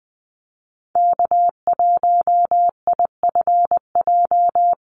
Morse code, heard as a keyed tone, about 20 wpm, 700 Hz. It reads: K1IFJ
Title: Call Signs - 20wpm